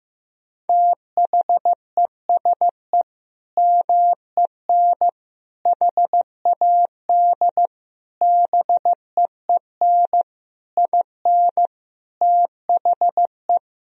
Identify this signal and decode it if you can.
Morse code, heard as a keyed tone, about 15 wpm, 700 Hz. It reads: THESE MEN HAD BEEN IN THE
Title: Sentences from Top 1000 Words - Review Part 1 - 15wpm